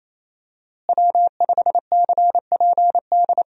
Morse code, heard as a keyed tone, about 28 wpm, 700 Hz. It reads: W5CPD
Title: USA Call Signs - 5 Characters - Repeated 3x - 28wpm